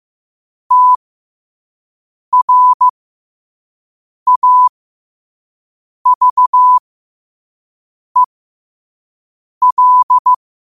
Morse code, heard as a keyed tone, about 15 wpm, spaced as for 5 wpm, 1000 Hz. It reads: TRAVEL
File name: 5wpm